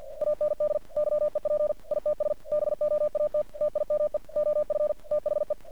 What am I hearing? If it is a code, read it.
Morse code, heard as a keyed tone, about 50 wpm, 600 Hz. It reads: AND YEP ITS BOAT TIME OF THE
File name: archived / CRSnet